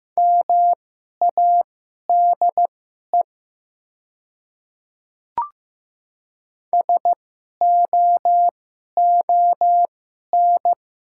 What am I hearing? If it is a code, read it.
Morse code, heard as a keyed tone, about 15 wpm, spaced as for 11 wpm, 700 Hz. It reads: MADE SOON
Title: 2x Character Spacing - Top 500 Words - 15wpm